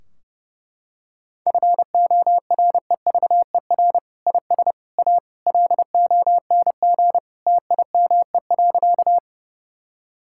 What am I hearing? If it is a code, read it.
Morse code, heard as a keyed tone, about 32 wpm, 700 Hz. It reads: FOREVER IS A LONG TIME.
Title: Sentences from Top 1600 Words 30wpm